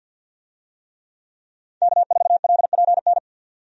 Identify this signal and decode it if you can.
Morse code, heard as a keyed tone, about 45 wpm, 700 Hz. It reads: K4LPN
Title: USA Call Signs - 5 Characters - 50wpm